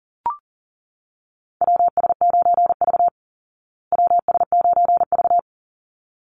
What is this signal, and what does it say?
Morse code, heard as a keyed tone, about 40 wpm, 700 Hz. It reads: WS9V WS9V
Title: Call Signs Repeated 3x - 40wpm